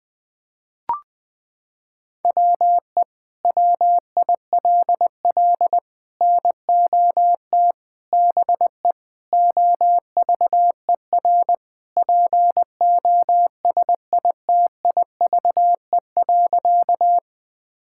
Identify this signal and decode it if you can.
Morse code, heard as a keyed tone, about 20 wpm, 700 Hz. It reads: WE WILL NOT BE OVER POSITIVE.
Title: Sentences from Top 1000 Words - Review Part 5 - 20wpm